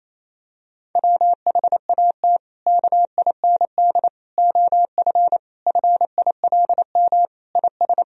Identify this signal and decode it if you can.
Morse code, heard as a keyed tone, about 28 wpm, 700 Hz. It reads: WHAT KIND OF FILM IS
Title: Sentences from Top 1000 Words - Review Part 3 - 28wpm